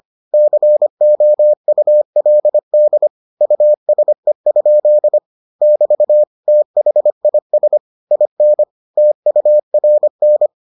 Morse code, heard as a keyed tone, about 25 wpm, 600 Hz. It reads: COULD USE? <BT> THIS IN TURN